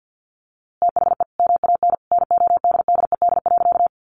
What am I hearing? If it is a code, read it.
Morse code, heard as a keyed tone, about 50 wpm, 700 Hz. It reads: THE MAN NODDED.
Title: A Sound Of Thunder 50wpm